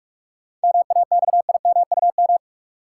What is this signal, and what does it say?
Morse code, heard as a keyed tone, about 46 wpm, 700 Hz. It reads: MAXIMUM
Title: Sentences from Top 1300 Words - New Word Drill Alt - 45wpm